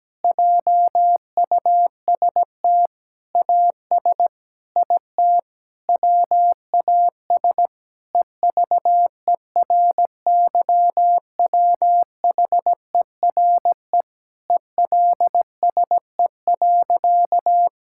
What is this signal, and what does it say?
Morse code, heard as a keyed tone, about 17 wpm, 700 Hz. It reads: JUST AS IT WAS EVERYWHERE ELSE.
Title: Sentences from Top 1600 Words 17wpm